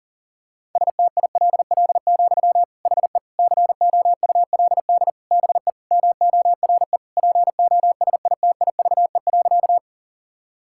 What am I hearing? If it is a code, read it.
Morse code, heard as a keyed tone, about 37 wpm, 700 Hz. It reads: STILL, HE COULD BE MORE POSITIVE.
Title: Sentences from Top 1000 Words - Review Part 5 - 40wpm